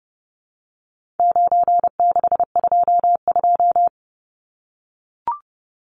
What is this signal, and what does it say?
Morse code, heard as a keyed tone, about 30 wpm, 700 Hz. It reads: 9622
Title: Numbers - 4-Digit 30wpm